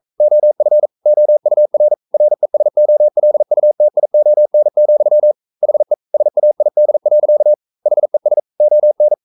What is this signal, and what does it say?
Morse code, heard as a keyed tone, about 42 wpm, 600 Hz. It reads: OF OUR RESOLUTION, HE SAID. HES ON
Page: Romanen Never af Ken Follet på engelsk.